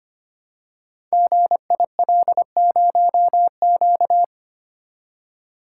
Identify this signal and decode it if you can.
Morse code, heard as a keyed tone, about 26 wpm, 700 Hz. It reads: GIL0Q
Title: Code Groups - 5 Letter:Numbers - 25wpm